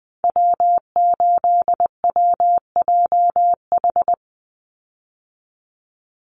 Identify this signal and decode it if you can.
Morse code, heard as a keyed tone, about 20 wpm, 700 Hz. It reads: W8WJH
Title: USA Call Signs 20wpm